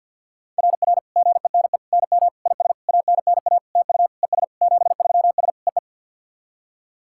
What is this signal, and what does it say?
Morse code, heard as a keyed tone, about 52 wpm, 690 Hz. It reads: RR GENE NM IS ANNA TU ES 73S EE
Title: SOTA QSO Practice 50wpm